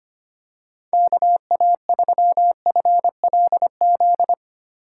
Morse code, heard as a keyed tone, about 25 wpm, 700 Hz. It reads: KA3FLZ
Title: USA Call Signs - 6 Characters - 3x Character Spacing - 25wpm